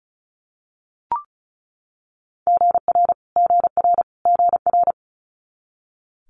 Morse code, heard as a keyed tone, about 36 wpm, 700 Hz. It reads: GR GR GR
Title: Multi-letter Phonemes Repeated 3x 35wpm